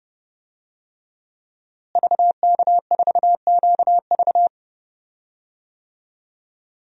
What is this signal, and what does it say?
Morse code, heard as a keyed tone, about 30 wpm, 700 Hz. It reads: VK4QV
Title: DXCC-100 Call Signs - 5 Characters - Repeated 3x - 30wpm